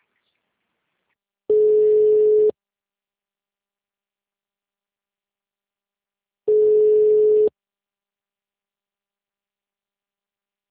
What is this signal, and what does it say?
Ringback tone (Europe, 425 Hz).